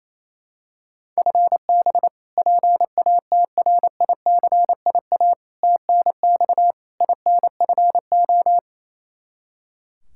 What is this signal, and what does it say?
Morse code, heard as a keyed tone, about 28 wpm, 700 Hz. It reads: FB PATRICIA TNX INFO
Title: Practice QSO - Exchange 3 - More Info - 28wpm